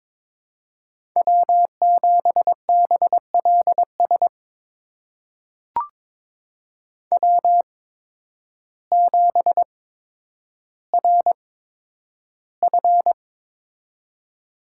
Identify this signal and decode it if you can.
Morse code, heard as a keyed tone, about 22 wpm, spaced as for 10 wpm, 700 Hz. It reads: W7BLS W7RF
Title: USA Call Signs - 8x Character Spacing - 22wpm